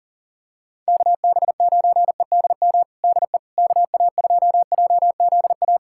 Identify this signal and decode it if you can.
Morse code, heard as a keyed tone, about 40 wpm, 700 Hz. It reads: KB9EDM DE KA2JZA
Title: USA Callsign DE Callsign 40wpm